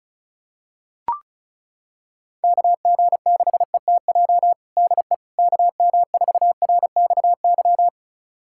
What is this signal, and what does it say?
Morse code, heard as a keyed tone, about 36 wpm, 700 Hz. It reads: KG6ETJ DE KM4RXY
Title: USA Callsign DE Callsign 35wpm